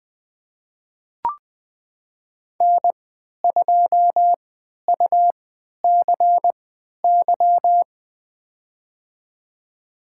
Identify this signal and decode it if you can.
Morse code, heard as a keyed tone, about 20 wpm, spaced as for 11 wpm, 700 Hz. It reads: N2UCY
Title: USA Call Signs - 5 Characters - 3x Character Spacing - 20wpm